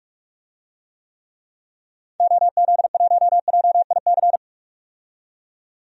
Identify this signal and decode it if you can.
Morse code, heard as a keyed tone, about 44 wpm, 700 Hz. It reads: OZ1JIC
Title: DXCC-100 Call Signs - 6 Characters - 45wpm